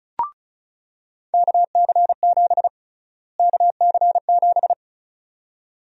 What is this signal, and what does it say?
Morse code, heard as a keyed tone, about 34 wpm, 700 Hz. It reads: KC7 KC7
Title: USA Call Sign Prefixes - Repeated 2x - 35wpm